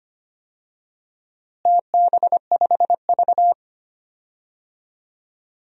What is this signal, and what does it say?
Morse code, heard as a keyed tone, about 25 wpm, 700 Hz. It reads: TB5V
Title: DXCC-100 Call Signs - 4 Characters - Repeated 2x - 25wpm